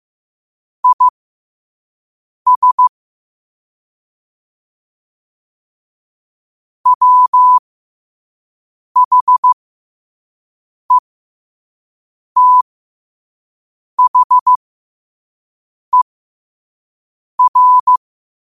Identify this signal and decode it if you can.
Morse code, heard as a keyed tone, about 15 wpm, spaced as for 5 wpm, 1000 Hz. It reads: IS WHETHER
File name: qotdfiles / 5wpm